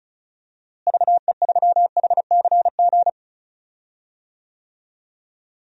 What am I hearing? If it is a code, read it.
Morse code, heard as a keyed tone, about 36 wpm, 700 Hz. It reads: VE3HCG
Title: DXCC-100 Call Signs - 6 Characters - Repeated 3x - 35wpm